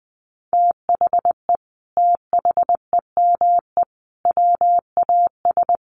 Morse code, heard as a keyed tone, about 19 wpm, 700 Hz. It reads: THE THEME WAS
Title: Sentences from Top 1100 Words 20wpm